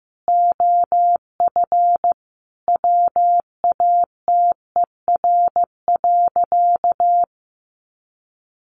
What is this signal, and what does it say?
Morse code, heard as a keyed tone, about 15 wpm, 700 Hz. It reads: OF WATER.
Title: Sentences from Top 700 Words 15wpm